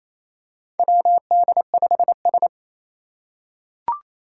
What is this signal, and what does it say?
Morse code, heard as a keyed tone, about 27 wpm, 700 Hz. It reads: WD5S
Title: Four Challenging Letter-Number Combinations 28wpm